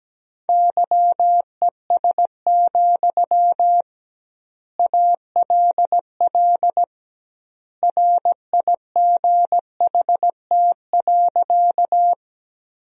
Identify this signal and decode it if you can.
Morse code, heard as a keyed tone, about 17 wpm, 700 Hz. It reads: YES, ALL RIGHT.
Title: 2x Word-Spacing - Sentences from Top 500 Words - 17wpm